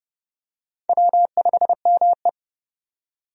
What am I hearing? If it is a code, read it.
Morse code, heard as a keyed tone, about 29 wpm, 700 Hz. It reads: W5ME
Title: USA Call Signs - 4 Characters - 30wpm